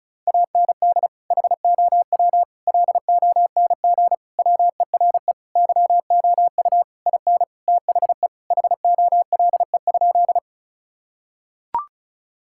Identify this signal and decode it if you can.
Morse code, heard as a keyed tone, about 36 wpm, 700 Hz. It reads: AND HOW LONG WERE YOU IN THE HOLE?
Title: Sentences from Top 1600 Words 35wpm